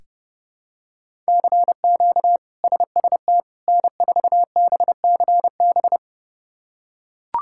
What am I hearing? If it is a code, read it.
Morse code, heard as a keyed tone, about 32 wpm, 700 Hz. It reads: CQ SST N4BCB
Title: K1USN SST QSO Practice - 30wpm